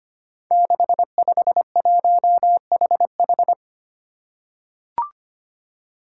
Morse code, heard as a keyed tone, about 25 wpm, 700 Hz. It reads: 651HH
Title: Five Challenging Letter-Number Combinations 25wpm